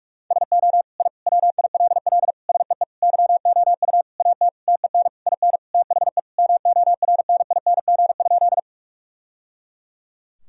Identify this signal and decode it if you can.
Morse code, heard as a keyed tone, about 46 wpm, 700 Hz. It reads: SO I WILL SEE YOU AT TEN IN THE MORNING?
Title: Sentences from Top 1000 Words - Review Part 2 - 45wpm